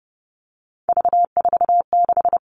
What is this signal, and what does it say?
Morse code, heard as a keyed tone, about 30 wpm, 700 Hz. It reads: V46
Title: All Cut Numbers - 3-Digit 30wpm